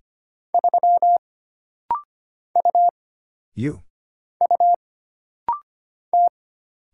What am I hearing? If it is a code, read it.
Morse code, heard as a keyed tone, about 25 wpm, 700 Hz. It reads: 3 U U T
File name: Single Character (Letter:Number:Punctuation:Prosign) - Rapid-Fire - 25wpm